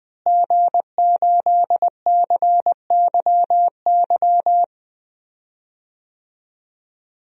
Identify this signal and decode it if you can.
Morse code, heard as a keyed tone, about 20 wpm, 700 Hz. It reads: G8CYY
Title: DXCC-100 Call Signs - 5 Characters - Repeated 3x - 20wpm